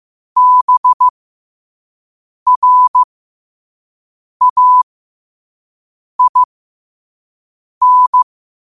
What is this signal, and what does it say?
Morse code, heard as a keyed tone, about 15 wpm, spaced as for 5 wpm, 1000 Hz. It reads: BRAIN
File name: qotdfiles / 5wpm